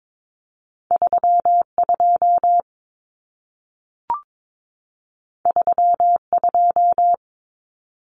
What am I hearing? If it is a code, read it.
Morse code, heard as a keyed tone, about 22 wpm, 700 Hz. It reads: 32 32
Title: Numbers - 2-Digit 22wpm